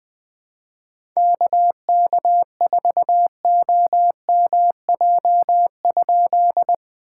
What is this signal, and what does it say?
Morse code, heard as a keyed tone, about 20 wpm, 700 Hz. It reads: KK4OMJ?
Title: POTA QSO Practice - 20wpm